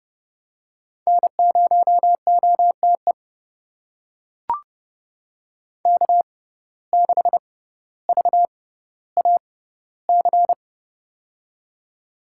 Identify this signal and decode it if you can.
Morse code, heard as a keyed tone, about 30 wpm, spaced as for 15 wpm, 700 Hz. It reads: N0OTE K6VAC
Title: USA Call Signs - 5 Characters - 6x Character Spacing - 30wpm